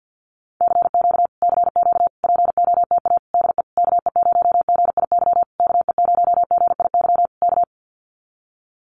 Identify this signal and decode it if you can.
Morse code, heard as a keyed tone, about 50 wpm, 700 Hz. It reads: CQ CQ POTA DE KE0GIY KE0GIY K
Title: Common QSO Phrases Encore 50wpm